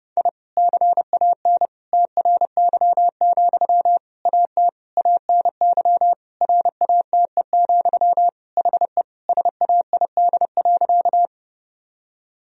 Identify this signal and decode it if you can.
Morse code, heard as a keyed tone, about 32 wpm, 700 Hz. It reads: I CAN TRY, AT ANY RATE, HE SAID.
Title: Sentences from Top 1000 Words - Review Part 4 - 30wpm